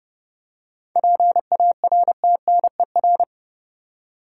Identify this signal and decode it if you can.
Morse code, heard as a keyed tone, about 31 wpm, 700 Hz. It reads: PARTNER
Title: Seven Letter Words - 30wpm